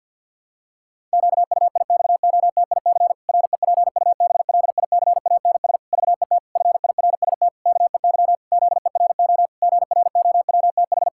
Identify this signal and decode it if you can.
Morse code, heard as a keyed tone, about 51 wpm, 700 Hz. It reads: QUIXOTIC REPUBLICANS VET FIRST KEY ZERO GROWTH
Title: Pangram Challenge v2 - 50wpm